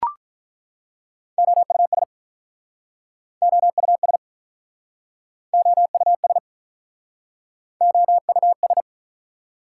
Morse code, heard as a keyed tone, about 48 wpm, 690 Hz. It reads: OUS OUS OUS OUS